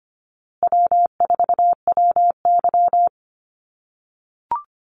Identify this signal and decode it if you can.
Morse code, heard as a keyed tone, about 25 wpm, 700 Hz. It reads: W4WY
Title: USA Call Signs Repeated 3x 25wpm